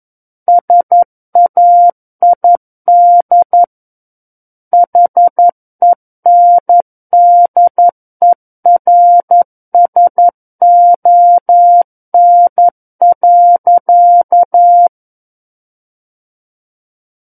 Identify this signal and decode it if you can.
Morse code, heard as a keyed tone, about 11 wpm, 700 Hz. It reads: SAID HENDERSON.
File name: 11 WPM